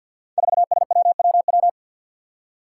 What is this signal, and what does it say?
Morse code, heard as a keyed tone, about 52 wpm, 700 Hz. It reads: 4SWWW
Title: Five Challenging Letter-Number Combinations 50wpm